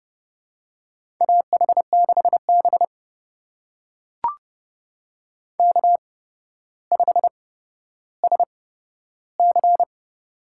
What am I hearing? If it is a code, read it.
Morse code, heard as a keyed tone, about 29 wpm, spaced as for 13 wpm, 700 Hz. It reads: AH6B K5SC
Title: USA Call Signs - 4 Characters - 8x Character Spacing - 30wpm